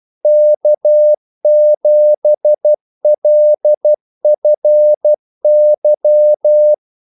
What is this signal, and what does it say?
Morse code, heard as a keyed tone, about 12 wpm, 600 Hz. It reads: K7LFY
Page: 2 ekstra ordmellemrum mellem kaldesignalerne